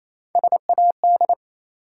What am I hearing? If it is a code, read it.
Morse code, heard as a keyed tone, about 28 wpm, 700 Hz. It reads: SAD
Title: Three Letter Words - 28wpm